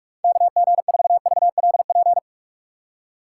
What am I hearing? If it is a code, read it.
Morse code, heard as a keyed tone, about 46 wpm, 700 Hz. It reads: KG4VLP
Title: USA Call Signs - 6 Characters - 4x Character Spacing - 45wpm